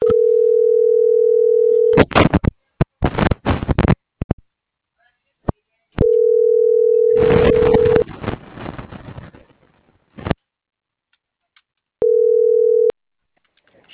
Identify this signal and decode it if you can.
Ringback tone (North America), 440 + 480 Hz.
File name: CENTRAL TERRESTRE / out karis